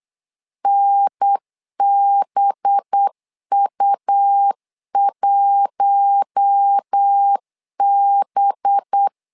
Morse code, heard as a keyed tone, about 8 wpm, 800 Hz. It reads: NBU1B
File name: LEZIONI / 5 LBVC / 5 LB 8wpm 2816